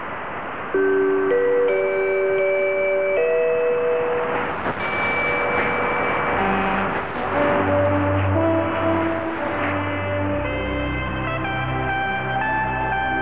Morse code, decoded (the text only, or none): none